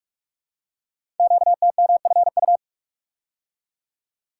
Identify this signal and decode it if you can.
Morse code, heard as a keyed tone, about 44 wpm, 700 Hz. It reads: QTMFU
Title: Code Groups - 5 Letters - 45wpm